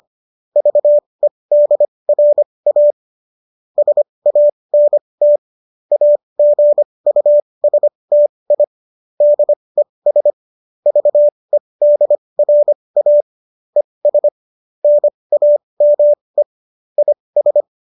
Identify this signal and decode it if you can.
Morse code, heard as a keyed tone, about 25 wpm, spaced as for 20 wpm, 590 Hz. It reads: VEDRA SANT AGUSTI DES VEDRA ES NAME IS
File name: qso214 20